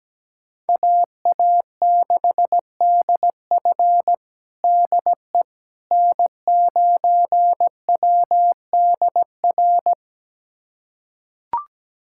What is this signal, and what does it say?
Morse code, heard as a keyed tone, about 17 wpm, 700 Hz. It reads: AA6DF DE N9WDR
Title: USA Callsign DE Callsign - 5 Characters - 17wpm